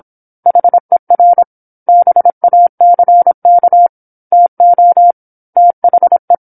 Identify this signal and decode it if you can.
Morse code, heard as a keyed tone, about 27 wpm, 700 Hz. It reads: HER BACK TO THE
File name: 08-Chapter-08 26 WPM